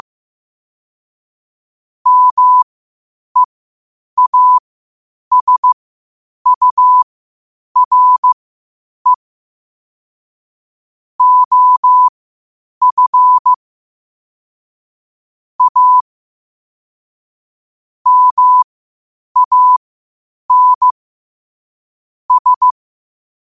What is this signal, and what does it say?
Morse code, heard as a keyed tone, about 15 wpm, spaced as for 8 wpm, 1000 Hz. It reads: MEASURE OF A MAN S
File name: qotdfiles / 8wpm